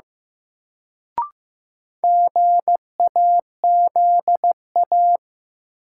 Morse code, heard as a keyed tone, about 15 wpm, 700 Hz. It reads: GAZA
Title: Sets of 5 Words - New Word Drill - 15wpm